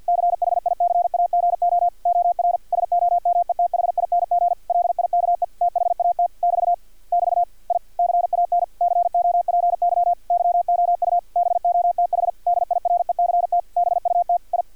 Morse code, heard as a keyed tone, about 50 wpm, 700 Hz, with radio noise. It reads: CHICAGO OR SOMETHING LIKE THAT <BT> <BT> I CAN COPY YOU BOTH DIRECT BUT I
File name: archived / CRSnet